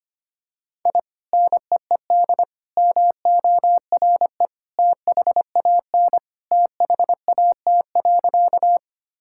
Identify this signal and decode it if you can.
Morse code, heard as a keyed tone, about 25 wpm, 700 Hz. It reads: I NEED MORE THAN THAT.
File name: practice / 25wpm / Sentences with 2-Word Phrases - 25wpm